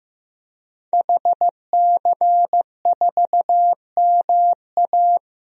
Morse code, heard as a keyed tone, about 15 wpm, 700 Hz. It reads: HC4MA
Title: DXCC-100 Call Signs - 5 Characters - Repeated 2x - 15wpm